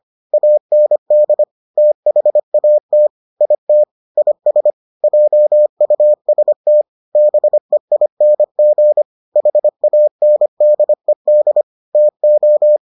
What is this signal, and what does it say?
Morse code, heard as a keyed tone, about 25 wpm, 600 Hz. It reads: AND THAT IT IS JUST BEING HANDED TO